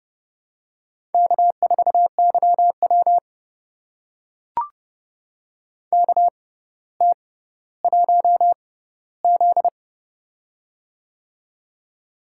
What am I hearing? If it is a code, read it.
Morse code, heard as a keyed tone, about 30 wpm, spaced as for 15 wpm, 700 Hz. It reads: K4YW KT1Z
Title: USA Call Signs - 4 Characters - 6x Character Spacing - 30wpm